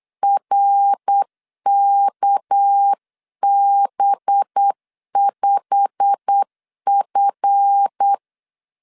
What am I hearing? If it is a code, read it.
Morse code, heard as a keyed tone, about 8 wpm, 800 Hz. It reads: RKB5F
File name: LEZIONI / 10 SIGNS / 10B INT 8wpm 3534